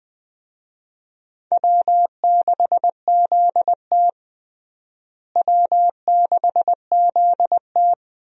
Morse code, heard as a keyed tone, about 20 wpm, 700 Hz. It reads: W6ZT W6ZT
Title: USA Call Signs - 4 Characters - Repeated 2x - 20wpm